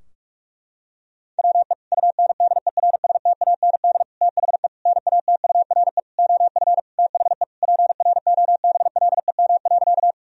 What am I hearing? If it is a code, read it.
Morse code, heard as a keyed tone, about 41 wpm, 700 Hz. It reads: WE UNDERSTAND THE NATURE OF THE PROBLEM.
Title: Sentences from Top 1000 Words - Review Part 2 - 45wpm